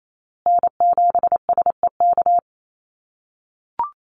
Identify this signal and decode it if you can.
Morse code, heard as a keyed tone, about 28 wpm, 700 Hz. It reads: N7SEK
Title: USA Call Signs Repeated 2x 28wpm